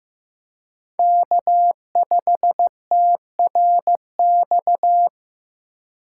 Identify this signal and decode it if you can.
Morse code, heard as a keyed tone, about 15 wpm, 700 Hz. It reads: K5TRX
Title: USA Call Signs - 5 Characters - 6x Character Spacing - 15wpm